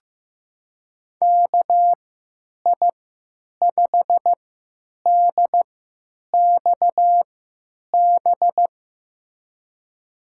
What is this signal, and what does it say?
Morse code, heard as a keyed tone, about 15 wpm, spaced as for 9 wpm, 700 Hz. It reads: KI5DXB
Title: USA Call Signs - 3x Character Spacing - 15wpm